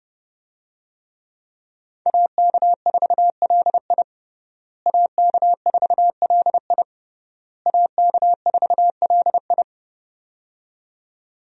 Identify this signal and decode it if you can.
Morse code, heard as a keyed tone, about 30 wpm, 700 Hz. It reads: AK4LI AK4LI AK4LI
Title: USA Call Signs - 5 Characters - Repeated 3x - 30wpm